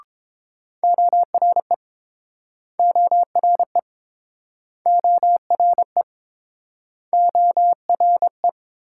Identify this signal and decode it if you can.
Morse code, heard as a keyed tone, about 31 wpm, 690 Hz. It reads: ORE ORE ORE ORE